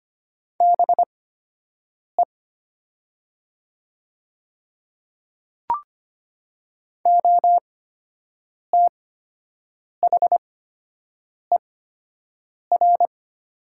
Morse code, heard as a keyed tone, about 25 wpm, spaced as for 7 wpm, 700 Hz. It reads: BE OTHER
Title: 8x Character Spacing - Top 100 Words - 25wpm